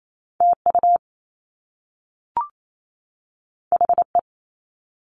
Morse code, heard as a keyed tone, about 28 wpm, 700 Hz. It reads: TU HE
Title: Top Two Letter Combinations in Words 28wpm